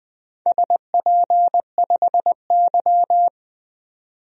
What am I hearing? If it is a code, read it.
Morse code, heard as a keyed tone, about 20 wpm, 700 Hz. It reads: SP5Y
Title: DXCC-100 Call Signs - 4 Characters - 20wpm